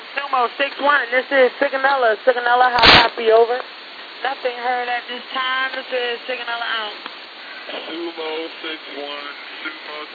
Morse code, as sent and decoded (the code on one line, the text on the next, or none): none